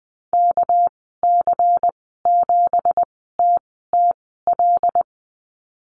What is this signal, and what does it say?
Morse code, heard as a keyed tone, about 20 wpm, spaced as for 15 wpm, 700 Hz. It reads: KC7TTL
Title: 219 Lesson 35g - Number 0 - Call Signs 20wpm